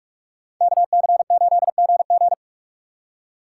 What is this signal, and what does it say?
Morse code, heard as a keyed tone, about 40 wpm, 700 Hz. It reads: KC8GG